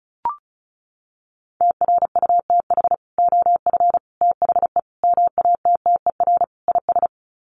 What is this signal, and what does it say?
Morse code, heard as a keyed tone, about 34 wpm, 700 Hz. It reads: TRUTH OF THE MATTER IS
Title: Sets of 5 Words Encore2 35wpm